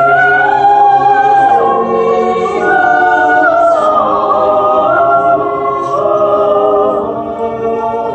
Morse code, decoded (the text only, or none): none